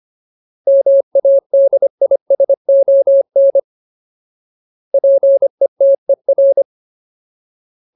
Morse code, heard as a keyed tone, about 25 wpm, 550 Hz. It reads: MADISON PETER
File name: Ultimate Word List / 25WPM Ultimate